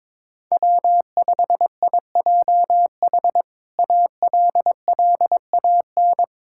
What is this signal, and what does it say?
Morse code, heard as a keyed tone, about 22 wpm, 700 Hz. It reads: W5IJH ALLAN
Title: K1USN SST QSO Practice - 22wpm